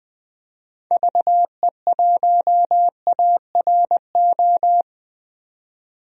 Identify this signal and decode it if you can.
Morse code, heard as a keyed tone, about 20 wpm, 700 Hz. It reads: VE1ARO
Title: DXCC-100 Call Signs - 6 Characters - 20wpm